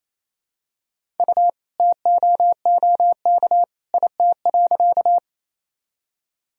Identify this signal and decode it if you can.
Morse code, heard as a keyed tone, about 28 wpm, 700 Hz. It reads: U TOOK IT.
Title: Sentences from Top 300 Words and Abbreviations 28wpm